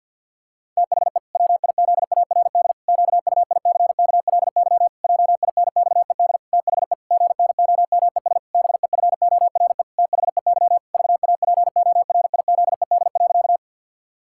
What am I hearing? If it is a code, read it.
Morse code, heard as a keyed tone, about 50 wpm, 700 Hz. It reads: THE WIZARD QUICKLY JINXED THE GNOMES BEFORE THEY VAPORIZED.
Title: Pangram Challenge v2 - 50wpm